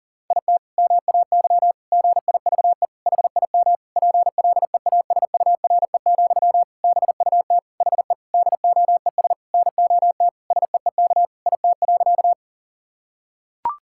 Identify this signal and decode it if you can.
Morse code, heard as a keyed tone, about 40 wpm, 700 Hz. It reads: IT MAY GIVE HIM PLEASURE, BUT HE DOES NOT SEEK IT.
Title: Sentences from Top 1600 Words 40wpm